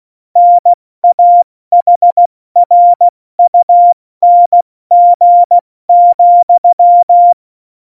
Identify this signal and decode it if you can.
Morse code, heard as a keyed tone, about 16 wpm, 700 Hz. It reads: NAHRUNG,
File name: Morsekurs / Witze / Witz 06. November 2018, 70 F80